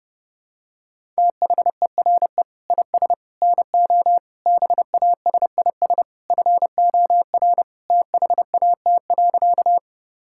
Morse code, heard as a keyed tone, about 30 wpm, 700 Hz. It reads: THERE IS NO BASIS FOR THAT.
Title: Sentences from Top 1000 Words - Review Part 5 - 30wpm